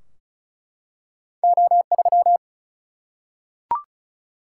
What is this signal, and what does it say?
Morse code, heard as a keyed tone, about 36 wpm, 700 Hz. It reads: O3
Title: Single-Letter Single-Number 35wpm